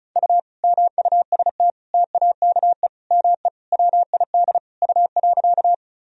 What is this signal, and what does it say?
Morse code, heard as a keyed tone, about 36 wpm, 700 Hz. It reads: U MUST TAKE ME WID U.
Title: Sentences from Top 200 Words and Abbreviations 35wpm